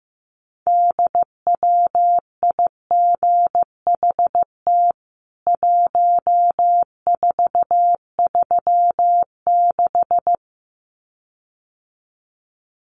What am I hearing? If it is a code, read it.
Morse code, heard as a keyed tone, about 15 wpm, 700 Hz. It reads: DWIGHT 1436
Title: CWT Exchange 15wpm